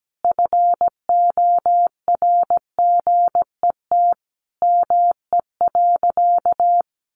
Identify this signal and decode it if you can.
Morse code, heard as a keyed tone, about 17 wpm, 700 Hz. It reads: FORGET ME.
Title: Sentences from Top 1200 Words 17wpm